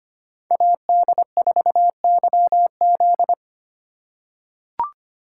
Morse code, heard as a keyed tone, about 25 wpm, 700 Hz. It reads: AD4YZ
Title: USA Call Signs - 5 Characters - 4x Character Spacing - 25wpm